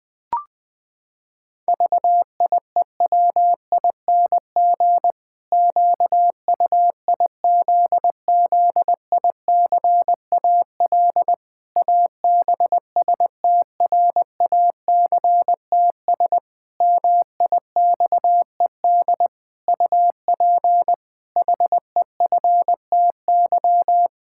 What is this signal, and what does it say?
Morse code, heard as a keyed tone, about 20 wpm, 700 Hz. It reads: VIEWING QUIZZICAL ABSTRACTS MIXED UP HEFTY
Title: Pangram Challenge v2 - 20wpm